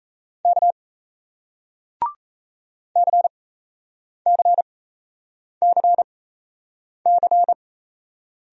Morse code, heard as a keyed tone, about 38 wpm, spaced as for 8 wpm, 700 Hz. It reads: K CCCC